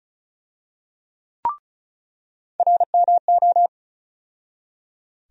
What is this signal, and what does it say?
Morse code, heard as a keyed tone, about 34 wpm, 700 Hz. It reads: RMO
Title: USA Call Sign Postfixes - 3 Letters 35wpm